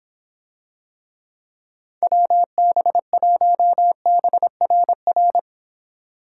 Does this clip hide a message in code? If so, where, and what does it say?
Yes: Morse code from 2.0 to 5.4 s, heard as a keyed tone, about 26 wpm, 700 Hz. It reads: WB1BRR